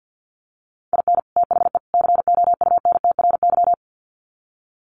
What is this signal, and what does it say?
Morse code, heard as a keyed tone, about 48 wpm, 700 Hz. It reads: IN THE COUNTRY
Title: Sets of 3 Words 50wpm